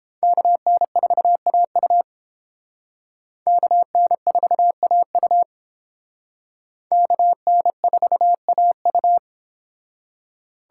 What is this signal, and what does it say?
Morse code, heard as a keyed tone, about 30 wpm, 700 Hz. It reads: KN4AU KN4AU KN4AU